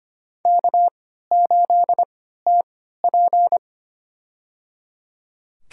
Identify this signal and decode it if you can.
Morse code, heard as a keyed tone, about 25 wpm, spaced as for 14 wpm, 700 Hz. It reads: K8TP
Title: USA Call Signs - 3x Character Spacing - 25wpm